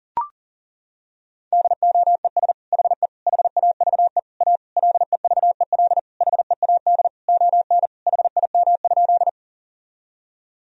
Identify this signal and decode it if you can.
Morse code, heard as a keyed tone, about 40 wpm, 700 Hz. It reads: DOES HE HAVE A LEVEL HEAD ON HIM?
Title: Sentences from Top 1000 Words - Review Part 4 - 40wpm